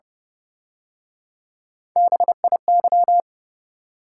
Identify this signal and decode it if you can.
Morse code, heard as a keyed tone, about 30 wpm, 700 Hz. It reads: BIY
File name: Speed-Racing - Three Letter-Number Combinations - 25wpm